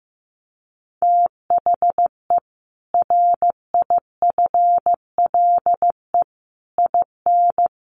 Morse code, heard as a keyed tone, about 15 wpm, 700 Hz. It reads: THE RIFLE IN
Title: A Sound Of Thunder 15wpm